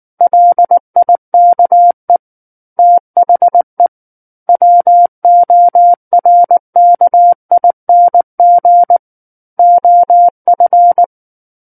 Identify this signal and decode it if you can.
Morse code, heard as a keyed tone, about 19 wpm, 700 Hz. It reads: LIKE THE WORKING OF
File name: Book One / 10-Chapter 10 - 19 WPM